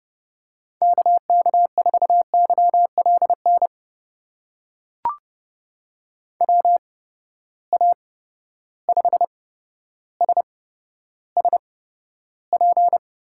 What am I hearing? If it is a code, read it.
Morse code, heard as a keyed tone, about 30 wpm, spaced as for 13 wpm, 700 Hz. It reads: KK4YLN WA5SSP
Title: USA Call Signs - 6 Characters - 8x Character Spacing - 30wpm